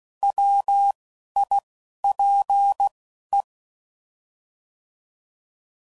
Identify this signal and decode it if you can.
Morse code, heard as a keyed tone, about 16 wpm, spaced as for 11 wpm, 800 Hz. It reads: WIPE